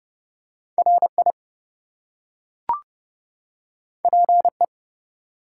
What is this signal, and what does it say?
Morse code, heard as a keyed tone, about 30 wpm, 700 Hz. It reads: RI PE
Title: US States and Canadian Provinces 30wpm